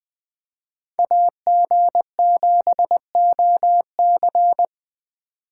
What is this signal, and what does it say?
Morse code, heard as a keyed tone, about 20 wpm, 700 Hz. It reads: AG7OC